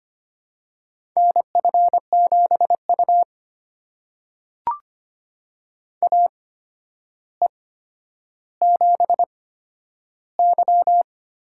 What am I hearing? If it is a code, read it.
Morse code, heard as a keyed tone, about 25 wpm, spaced as for 11 wpm, 700 Hz. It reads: NF7U AE7Y
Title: USA Call Signs - 8x Character Spacing - 25wpm